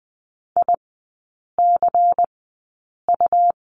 Morse code, heard as a keyed tone, about 20 wpm, spaced as for 8 wpm, 690 Hz. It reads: ICU
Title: 145 Lesson 24e - Letter y - Three Characters 20wpm